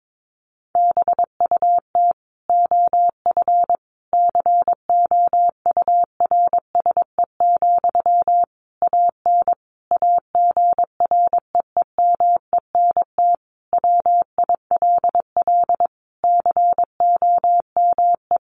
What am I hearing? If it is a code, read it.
Morse code, heard as a keyed tone, about 22 wpm, 700 Hz. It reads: BUT OF COURSE, AN AGREEMENT WILL COME
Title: Sentences from Top 1500 Words 22wpm